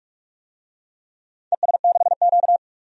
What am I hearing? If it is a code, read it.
Morse code, heard as a keyed tone, about 44 wpm, 700 Hz. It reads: ES6Q
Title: DXCC-100 Call Signs - 4 Characters - 45wpm